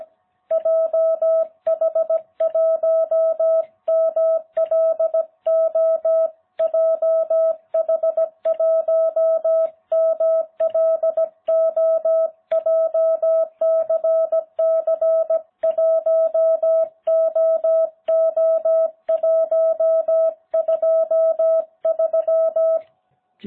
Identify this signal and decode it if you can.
Morse code, heard as a keyed tone, about 17 wpm, 660 Hz. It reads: JH1MLO JH1MLO JCC1OO123